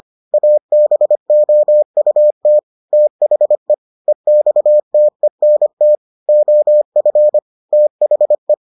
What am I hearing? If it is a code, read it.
Morse code, heard as a keyed tone, about 25 wpm, 600 Hz. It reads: ABOUT THE EXTENT OF THE